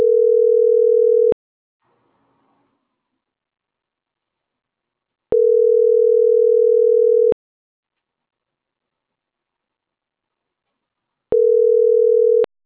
Ringback tone (North America), 440 + 480 Hz.